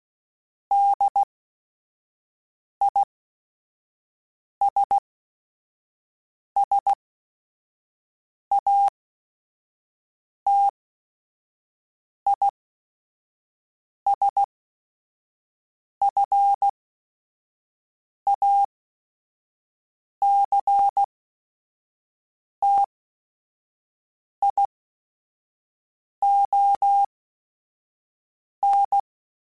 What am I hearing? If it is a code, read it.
Morse code, heard as a keyed tone, about 16 wpm, spaced as for 5 wpm, 800 Hz. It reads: DISSATISFACTION